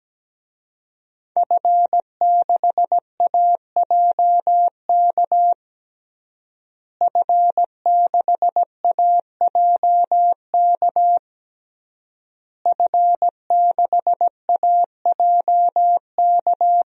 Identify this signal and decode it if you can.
Morse code, heard as a keyed tone, about 17 wpm, 700 Hz. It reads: F6AJK F6AJK F6AJK
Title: DXCC-100 Call Signs - 5 Characters - Repeated 3x - 17wpm